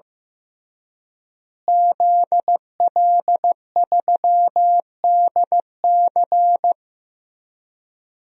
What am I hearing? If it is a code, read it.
Morse code, heard as a keyed tone, about 15 wpm, 700 Hz. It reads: ZL3DC
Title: DXCC-100 Call Signs - 5 Characters - Repeated 3x - 15wpm